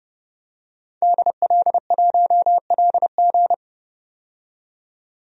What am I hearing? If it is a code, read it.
Morse code, heard as a keyed tone, about 30 wpm, 700 Hz. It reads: DL1LG
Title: DXCC-100 Call Signs - 5 Characters - 30wpm